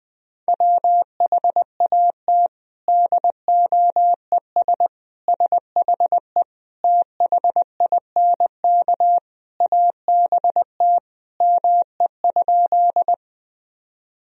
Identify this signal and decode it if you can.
Morse code, heard as a keyed tone, about 20 wpm, 700 Hz. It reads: WHAT DOES SHE THINK ABT ME?
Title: Sentences from Top 200 Words and Abbreviations 20wpm